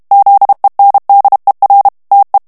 Morse code, heard as a keyed tone, about 32 wpm, 800 Hz. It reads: ZENDER TE
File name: morse mp3 / HOLLANDAIS texte clair 1 vitesse 30 mots minute